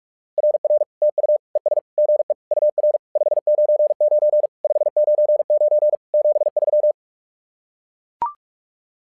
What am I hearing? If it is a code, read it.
Morse code, heard as a keyed tone, about 44 wpm, 600 Hz. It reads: RR TU ES GE UR 599 599 73
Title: SOTA QSO Practice 45wpm